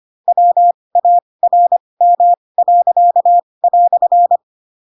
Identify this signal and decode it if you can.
Morse code, heard as a keyed tone, about 25 wpm, spaced as for 20 wpm, 700 Hz. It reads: WARM."
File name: Pinocchio / Pinocchio ch8 15 25 ik1hge